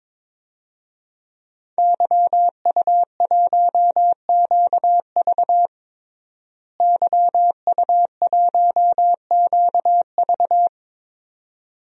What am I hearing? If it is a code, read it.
Morse code, heard as a keyed tone, about 22 wpm, 700 Hz. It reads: YU1QV YU1QV
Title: DXCC-100 Call Signs - 5 Characters - Repeated 2x - 22wpm